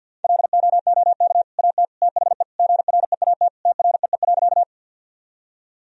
Morse code, heard as a keyed tone, about 52 wpm, 700 Hz. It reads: LOOK AT THE GREAT TREE.
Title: Sentences from Top 1000 Words - Review Part 1 - 50wpm